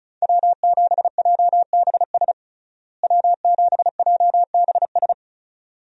Morse code, heard as a keyed tone, about 36 wpm, 700 Hz. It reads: W7JBS W7JBS
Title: USA Call Signs - 5 Characters - Repeated 2x - 35wpm